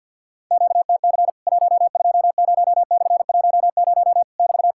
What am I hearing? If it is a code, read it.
Morse code, heard as a keyed tone, about 50 wpm, 700 Hz. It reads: QTC 120/10 <BT>
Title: Worked All Europe DX Contest - QTC Exchange with Repeats - 50wpm